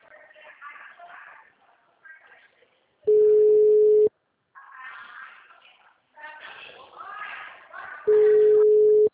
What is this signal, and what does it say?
Ringback tone (Europe, 425 Hz), with speech.
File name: out- -2026- - - .2220